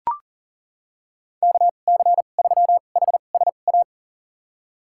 Morse code, heard as a keyed tone, about 42 wpm, spaced as for 29 wpm, 690 Hz. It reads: KC3HSA
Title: USA Call Signs - 6 Characters - 2x Character Spacing - 40wpm